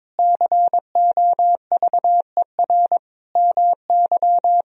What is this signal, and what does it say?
Morse code, heard as a keyed tone, about 22 wpm, 700 Hz. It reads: COVER MY
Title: Sentences from Top 1000 Words - Review Part 1 - 22wpm